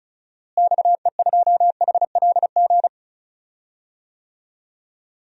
Morse code, heard as a keyed tone, about 34 wpm, 700 Hz. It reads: XE2HLG
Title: DXCC-100 Call Signs - 6 Characters - Repeated 2x - 35wpm